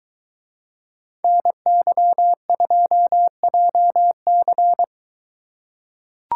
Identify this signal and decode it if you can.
Morse code, heard as a keyed tone, about 22 wpm, 700 Hz. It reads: NY2JC